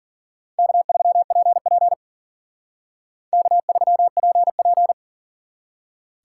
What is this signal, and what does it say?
Morse code, heard as a keyed tone, about 51 wpm, 700 Hz. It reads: K3PP K3PP